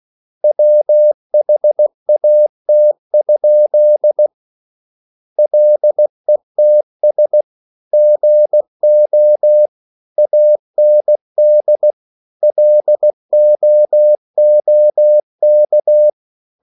Morse code, heard as a keyed tone, about 16 wpm, 600 Hz. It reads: WHAT? LETS GO AND LOOK